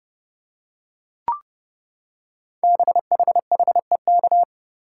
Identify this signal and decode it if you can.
Morse code, heard as a keyed tone, about 28 wpm, 700 Hz. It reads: BHHEK
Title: Code Groups - 5 Letters - 30wpm